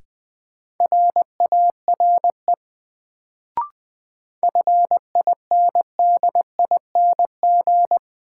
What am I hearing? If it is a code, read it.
Morse code, heard as a keyed tone, about 20 wpm, 700 Hz. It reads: RARE FINDING
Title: Sentences from Top 1100 Words - New Word Drill Alt - 20wpm